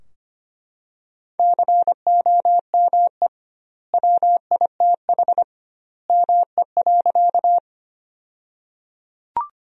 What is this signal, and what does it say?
Morse code, heard as a keyed tone, about 25 wpm, 700 Hz. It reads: COME WITH ME.
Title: 2x Word-Spacing - Sentences from Top 200 Words - 25wpm